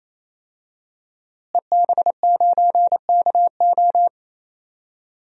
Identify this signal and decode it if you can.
Morse code, heard as a keyed tone, about 27 wpm, 700 Hz. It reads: EB9KO
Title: DXCC-100 Call Signs - 5 Characters - Repeated 3x - 28wpm